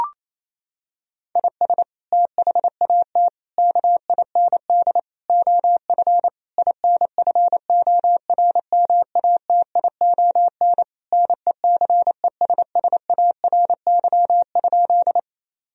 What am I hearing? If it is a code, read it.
Morse code, heard as a keyed tone, about 28 wpm, 700 Hz. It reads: IS THAT KIND OF INFORMATION NECESSARY?
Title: Sentences from Top 1000 Words - Review Part 3 - 28wpm